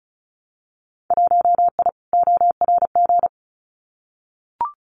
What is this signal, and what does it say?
Morse code, heard as a keyed tone, about 34 wpm, 690 Hz. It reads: 1I ORG
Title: Winter Field Day Exchanges 35wpm